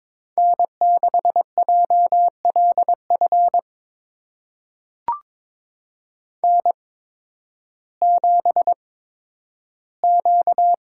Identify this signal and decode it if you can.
Morse code, heard as a keyed tone, about 22 wpm, spaced as for 12 wpm, 700 Hz. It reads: N6JLF N7Q
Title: USA Call Signs - 5 Characters - 8x Character Spacing - 22wpm